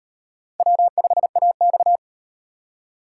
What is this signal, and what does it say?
Morse code, heard as a keyed tone, about 38 wpm, 700 Hz. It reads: W5AX